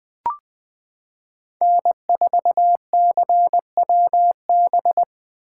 Morse code, heard as a keyed tone, about 20 wpm, 700 Hz. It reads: N4CWB
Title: 2023 POTA Hunter Call Signs - 20wpm